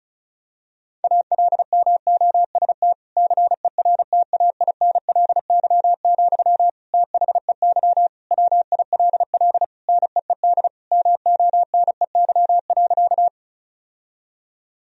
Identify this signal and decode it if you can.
Morse code, heard as a keyed tone, about 34 wpm, 700 Hz. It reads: ALMOST CERTAINLY, THEY WILL NEED MONEY.
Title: Sentences from Top 1000 Words - Review Part 4 - 35wpm